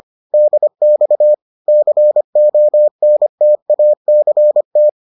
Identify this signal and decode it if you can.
Morse code, heard as a keyed tone, about 25 wpm, 600 Hz. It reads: DX CONTACT